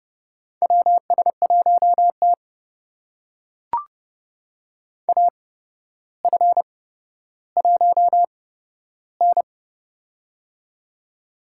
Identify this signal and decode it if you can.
Morse code, heard as a keyed tone, about 30 wpm, spaced as for 13 wpm, 700 Hz. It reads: WS1T AF1N
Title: USA Call Signs - 4 Characters - 8x Character Spacing - 30wpm